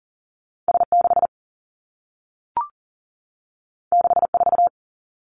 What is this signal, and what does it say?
Morse code, heard as a keyed tone, about 39 wpm, 700 Hz. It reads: S6 64
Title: Two Challenging Letter-Number Combinations 40wpm